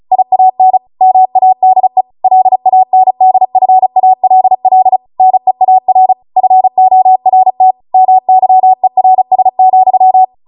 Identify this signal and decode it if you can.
Morse code, heard as a keyed tone, about 36 wpm, 750 Hz. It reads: IAN MADE LANDFALL NEAR FORT MYERS,